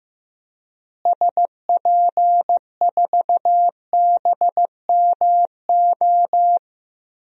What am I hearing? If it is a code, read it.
Morse code, heard as a keyed tone, about 15 wpm, 700 Hz. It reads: SP4BMO
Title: DXCC-100 Call Signs - 6 Characters - Repeated 3x - 15wpm